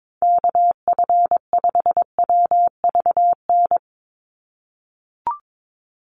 Morse code, heard as a keyed tone, about 22 wpm, 700 Hz. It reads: KF5WVN
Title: USA Call Signs 22wpm